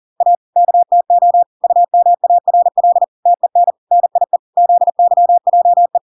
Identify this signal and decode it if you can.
Morse code, heard as a keyed tone, about 40 wpm, 700 Hz. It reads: A KTO UMARL TEN NIE ZYJE